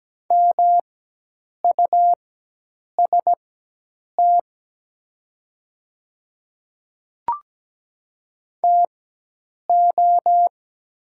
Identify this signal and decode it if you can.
Morse code, heard as a keyed tone, about 17 wpm, spaced as for 8 wpm, 700 Hz. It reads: MUST TO
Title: 4x Character Spacing - Top 200 Words - 17wpm